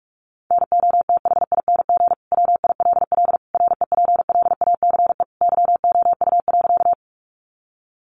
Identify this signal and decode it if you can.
Morse code, heard as a keyed tone, about 44 wpm, 700 Hz. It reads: NOTHING WILL REPLACE YOU.
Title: Sentences from Top 1400 Words 45wpm